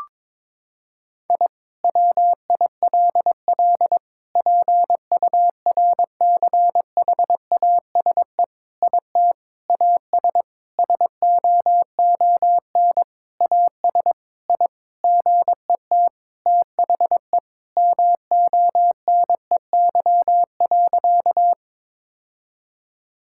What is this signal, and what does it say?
Morse code, heard as a keyed tone, about 22 wpm, 700 Hz. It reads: I WILL PURCHASE IT AS SOON AS I GET THE MONEY.
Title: Sentences from Top 1000 Words - Review Part 5 - 22wpm